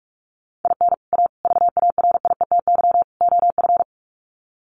Morse code, heard as a keyed tone, about 44 wpm, 700 Hz. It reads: IN A VARIETY OF
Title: Sets of 4 Words 45wpm